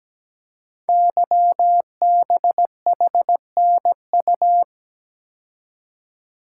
Morse code, heard as a keyed tone, about 17 wpm, 700 Hz. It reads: YBHNU
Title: Code Groups - 5 Letter:Numbers - 17wpm